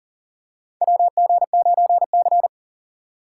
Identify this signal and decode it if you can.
Morse code, heard as a keyed tone, about 40 wpm, 700 Hz. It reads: WG9C
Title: USA Call Signs - 4 Characters - 3x Character Spacing - 40wpm